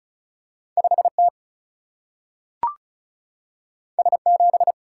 Morse code, heard as a keyed tone, about 36 wpm, 700 Hz. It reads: 5 T S 7
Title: Single-Letter Single-Number 35wpm